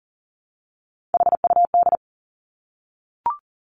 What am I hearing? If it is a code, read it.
Morse code, heard as a keyed tone, about 41 wpm, 690 Hz. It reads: HUD
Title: Three Challenging Letter-Number Combinations 40wpm